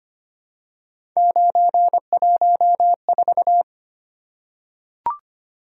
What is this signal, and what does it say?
Morse code, heard as a keyed tone, about 25 wpm, 700 Hz. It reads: 914
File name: Three Letter-Number Combinations - 25wpm